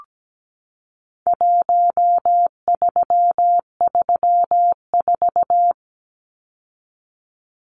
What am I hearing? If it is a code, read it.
Morse code, heard as a keyed tone, about 17 wpm, 700 Hz. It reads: 1334
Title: Numbers - Contest Style - 4-Digit 17wpm